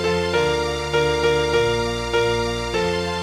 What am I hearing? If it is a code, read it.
There is no code.